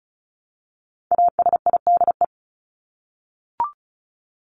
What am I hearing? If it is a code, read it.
Morse code, heard as a keyed tone, about 34 wpm, 700 Hz. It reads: ASIDE
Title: Top 1751-2000 Words 35wpm